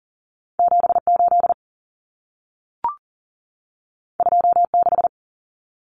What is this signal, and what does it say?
Morse code, heard as a keyed tone, about 40 wpm, 700 Hz. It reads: 78 26
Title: Common Cut Numbers - 2-Digit 40wpm